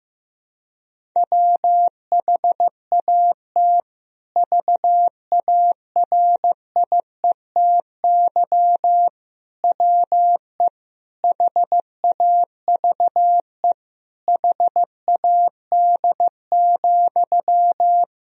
Morse code, heard as a keyed tone, about 15 wpm, 700 Hz. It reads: WHAT VARIETY WE HAVE HAD,
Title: Sentences from Top 1000 Words - Review Part 4 - 15wpm